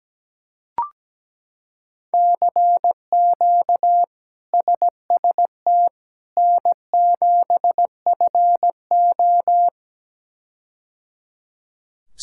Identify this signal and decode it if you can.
Morse code, heard as a keyed tone, about 17 wpm, 700 Hz. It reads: CQ SST N7FO
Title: K1USN SST QSO Practice - 17wpm